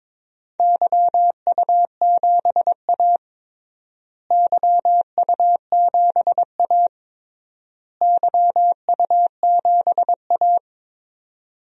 Morse code, heard as a keyed tone, about 22 wpm, 700 Hz. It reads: YU7A YU7A YU7A
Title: DXCC-100 Call Signs - 4 Characters - Repeated 3x - 22wpm